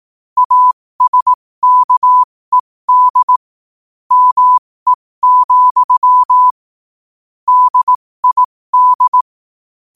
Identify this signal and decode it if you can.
Morse code, heard as a keyed tone, about 18 wpm, spaced as for 15 wpm, 1000 Hz. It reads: ASKED ME, DID
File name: qotdfiles / 15wpm